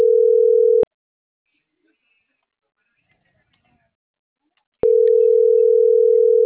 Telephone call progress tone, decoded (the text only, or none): ringback tone